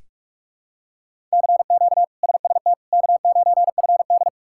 Morse code, heard as a keyed tone, about 42 wpm, 700 Hz. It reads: CQ SST K9FD
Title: K1USN SST QSO Practice - 45wpm